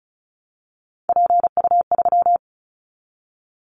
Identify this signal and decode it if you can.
Morse code, heard as a keyed tone, about 34 wpm, 700 Hz. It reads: PU3
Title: Callsign Prefixes for Top 100 DXCC - 35wpm